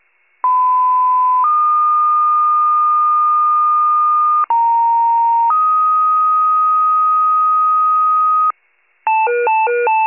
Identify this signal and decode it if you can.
Two-tone page, heard as a one-second tone, then a three-second tone, about 1010 then 1220 Hz, about 930 then 1250 Hz. It is heard from 0.4 to 8.5 s.